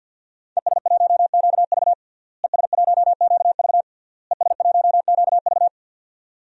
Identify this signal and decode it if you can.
Morse code, heard as a keyed tone, about 51 wpm, 700 Hz. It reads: ES1QV ES1QV ES1QV
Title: DXCC-100 Call Signs - 5 Characters - Repeated 3x - 50wpm